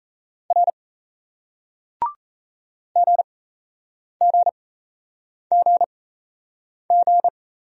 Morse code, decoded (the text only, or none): R GGGG